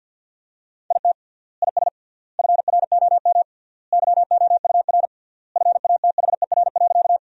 Morse code, heard as a keyed tone, about 53 wpm, 700 Hz. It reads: IT IS FROM YOUR FATHER.
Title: 3x Word-Spacing - Sentences from Top 300 Words - 50wpm